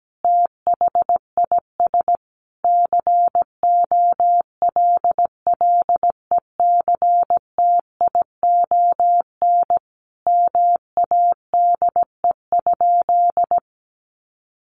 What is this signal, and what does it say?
Morse code, heard as a keyed tone, about 17 wpm, 700 Hz. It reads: THIS COLLECTION MADE?
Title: Sentences from Top 1200 Words 17wpm